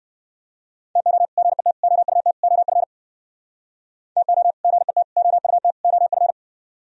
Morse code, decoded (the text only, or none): TO GET OUT OF TO GET OUT OF